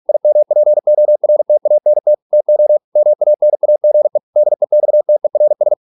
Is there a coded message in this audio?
Yes: Morse code from 0.1 to 5.7 s, heard as a keyed tone, about 46 wpm, 600 Hz. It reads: IMPORTANT TO MANAGE DEXTERS